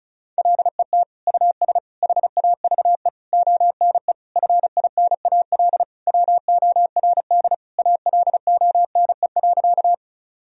Morse code, heard as a keyed tone, about 36 wpm, 700 Hz. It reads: LET US HAVE ONE FINAL WORD ALONE.
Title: Sentences from Top 1000 Words - Review Part 2 - 35wpm